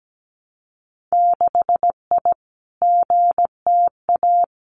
Morse code, heard as a keyed tone, about 17 wpm, 700 Hz. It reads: 6I GTA
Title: Winter Field Day Exchanges Encore 17wpm